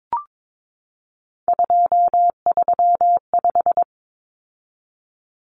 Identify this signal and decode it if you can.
Morse code, heard as a keyed tone, about 22 wpm, 700 Hz. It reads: 235
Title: Common Cut Numbers - 3-Digit 22wpm